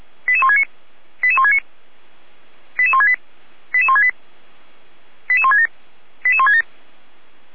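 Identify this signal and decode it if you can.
Five-tone selective call (ZVEI), heard as quick runs of five tones, tones of about 70 ms. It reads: 88169 88169 88168 88168 88167 88167
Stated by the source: ZVEI five-tone selective call received on SDR.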